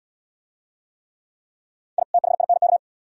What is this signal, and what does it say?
Morse code, heard as a keyed tone, about 66 wpm, 700 Hz. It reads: I THINK